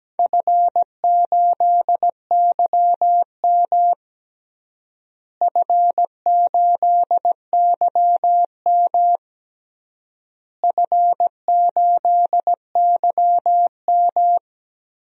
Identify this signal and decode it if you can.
Morse code, heard as a keyed tone, about 17 wpm, 700 Hz. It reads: F8YM F8YM F8YM
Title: DXCC-100 Call Signs - 4 Characters - Repeated 3x - 17wpm